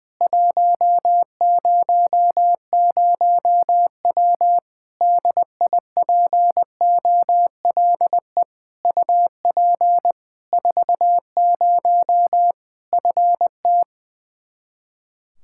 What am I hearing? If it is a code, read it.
Morse code, heard as a keyed tone, about 20 wpm, 700 Hz. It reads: 100W DIPOLE UP 40 FT
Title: Practice QSO - Exchange 2 - Info - 20wpm